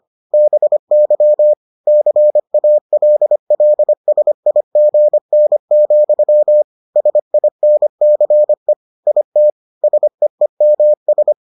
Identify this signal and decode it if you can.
Morse code, heard as a keyed tone, about 25 wpm, 600 Hz. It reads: BY CALLSIGN, SINCE IT SEEMS